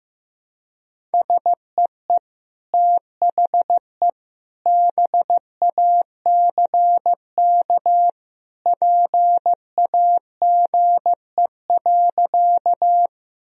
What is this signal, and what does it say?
Morse code, heard as a keyed tone, about 16 wpm, 700 Hz. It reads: SEE THE BACK PAGE.
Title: Sentences from Top 1000 Words - Review Part 1 - 15wpm